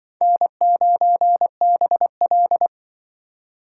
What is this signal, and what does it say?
Morse code, heard as a keyed tone, about 25 wpm, 700 Hz. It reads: N9BL
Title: Speed Racing - USA 1x2 Call Signs 20wpm